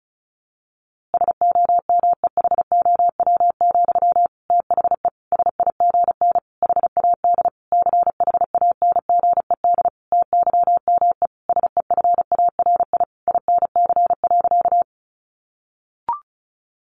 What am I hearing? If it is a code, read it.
Morse code, heard as a keyed tone, about 34 wpm, 700 Hz. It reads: SOMEHOW, THE SIGN HAD CHANGED TYME SEFARI INC.
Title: A Sound Of Thunder 35wpm